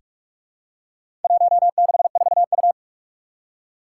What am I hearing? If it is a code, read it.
Morse code, heard as a keyed tone, about 46 wpm, 700 Hz. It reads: 164U
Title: Four Challenging Letter-Number Combinations 45wpm